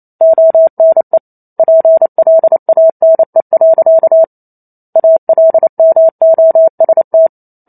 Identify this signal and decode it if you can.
Morse code, heard as a keyed tone, about 28 wpm, 650 Hz. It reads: ONE PLANE. ALMOST
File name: Chapter 2 - 28 WPM